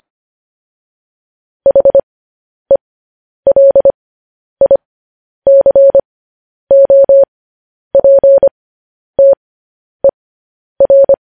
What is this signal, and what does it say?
Morse code, heard as a keyed tone, about 25 wpm, spaced as for 10 wpm, 550 Hz. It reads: HELICOPTER
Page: news headlines in morse code at 10 wpm